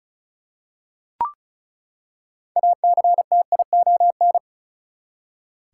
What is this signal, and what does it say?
Morse code, heard as a keyed tone, about 34 wpm, 700 Hz. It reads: ACTION
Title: Six Letter Words - 35wpm